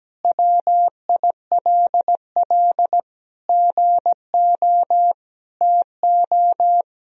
Morse code, heard as a keyed tone, about 17 wpm, 700 Hz. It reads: WILL GO TO
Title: Sentences from Top 1000 Words - Review Part 1 - 17wpm